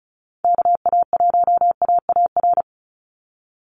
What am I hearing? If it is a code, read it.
Morse code, heard as a keyed tone, about 34 wpm, 700 Hz. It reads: KA1AAR
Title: USA Call Signs 35wpm